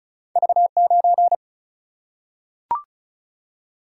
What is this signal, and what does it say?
Morse code, heard as a keyed tone, about 36 wpm, 700 Hz. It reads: V 9
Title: Single-Letter Single-Number 35wpm